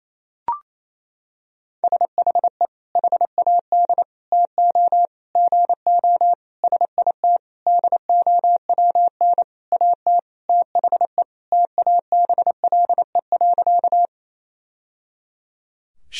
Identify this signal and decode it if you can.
Morse code, heard as a keyed tone, about 28 wpm, 700 Hz. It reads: SHE HAD TO GO SIT DOWN AT THE TABLE.
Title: Sentences from Top 1000 Words - Review Part 2 - 28wpm